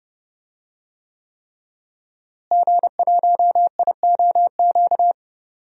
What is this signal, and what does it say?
Morse code, heard as a keyed tone, about 30 wpm, 700 Hz. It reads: G1IOQ
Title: DXCC-100 Call Signs - 5 Characters - Repeated 3x - 30wpm